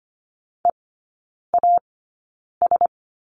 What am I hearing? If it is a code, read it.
Morse code, heard as a keyed tone, about 25 wpm, spaced as for 9 wpm, 700 Hz. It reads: EAS
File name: course / 25:10wpm / 093 Lesson 16e - Letter c - Three Characters 25wpm